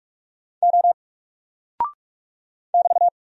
Morse code, heard as a keyed tone, about 44 wpm, 700 Hz. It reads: O <BT>
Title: Single Character (Letter:Number:Punctuation:Prosign) Alt 45wpm